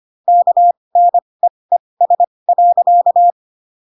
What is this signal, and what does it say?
Morse code, heard as a keyed tone, about 25 wpm, spaced as for 20 wpm, 700 Hz. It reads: KNEES.
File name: Pinocchio / Pinocchio ch2 15 25 ik1hge